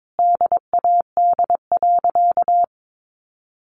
Morse code, heard as a keyed tone, about 22 wpm, 700 Hz. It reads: DAD.
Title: Additional Easy Phrases 22wpm